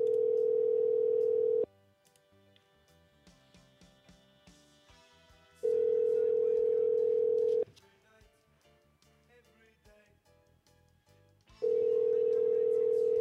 Ringback tone (North America), 440 + 480 Hz.